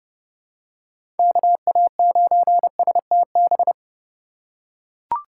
Morse code, heard as a keyed tone, about 30 wpm, 700 Hz. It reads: KA9STB
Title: USA Call Signs - 3x Character Spacing - 30wpm